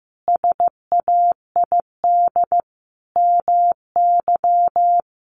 Morse code, heard as a keyed tone, about 15 wpm, 700 Hz. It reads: SAID MY
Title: Sentences from Top 1400 Words 15wpm